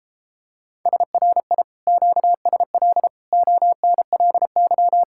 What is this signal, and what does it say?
Morse code, heard as a keyed tone, about 29 wpm, 700 Hz. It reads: SRI QSL ONLY